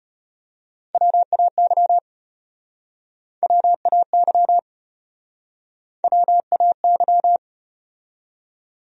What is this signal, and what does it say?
Morse code, heard as a keyed tone, about 33 wpm, 700 Hz. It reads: WAY WAY WAY